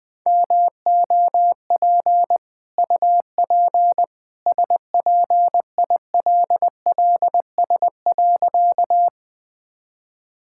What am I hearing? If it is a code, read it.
Morse code, heard as a keyed tone, about 20 wpm, 700 Hz. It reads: MOP UP SPILLS.
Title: Easy Sentences with Hints - 20wpm